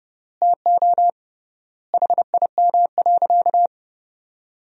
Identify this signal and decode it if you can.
Morse code, heard as a keyed tone, about 32 wpm, 700 Hz. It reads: TO HIM.
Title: 3x Word-Spacing - Sentences from Top 300 Words - 30wpm